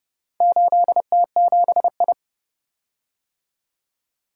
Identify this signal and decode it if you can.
Morse code, heard as a keyed tone, about 29 wpm, 700 Hz. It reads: 8T7I